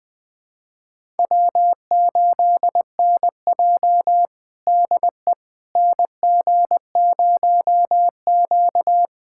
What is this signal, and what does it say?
Morse code, heard as a keyed tone, about 20 wpm, 700 Hz. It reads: W8NJ DE NG0Q
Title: USA Callsign DE Callsign - 4 Characters - 20wpm